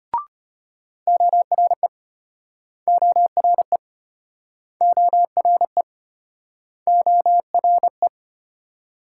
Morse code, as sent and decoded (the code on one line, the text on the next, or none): --- .-. / --- .-. / --- .-. / --- .-. .
ORE ORE ORE ORE